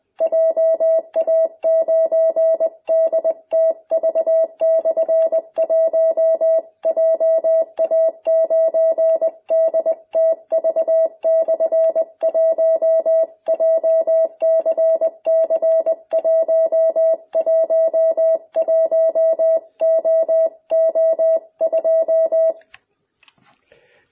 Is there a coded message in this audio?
Yes: Morse code from 0.2 to 22.5 s, heard as a keyed tone, about 20 wpm, 650 Hz. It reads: JA9DTV/1 JA9DTV/1 JCC111OO2